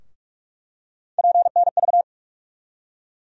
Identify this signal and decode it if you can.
Morse code, heard as a keyed tone, about 46 wpm, 700 Hz. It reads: PNV
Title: USA Call Sign Postfixes - 3 Letters 45wpm